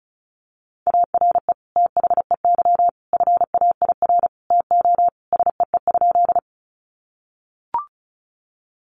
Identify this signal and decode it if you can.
Morse code, heard as a keyed tone, about 34 wpm, 700 Hz. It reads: ARE THEY FAIR TO SEE?
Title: Sentences from Top 1300 Words 35wpm